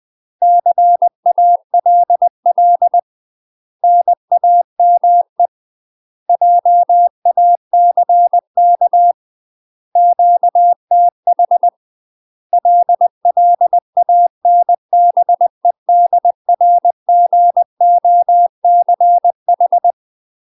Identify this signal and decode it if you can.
Morse code, heard as a keyed tone, about 20 wpm, 700 Hz. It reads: CALL NAME JACK QTH LLANBEDRGOCH
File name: UK MASTER QSOs TRAINING/20WPM